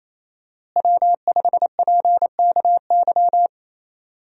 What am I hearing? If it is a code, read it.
Morse code, heard as a keyed tone, about 28 wpm, 700 Hz. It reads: W5PKY
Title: USA Call Signs - 5 Characters - Repeated 3x - 28wpm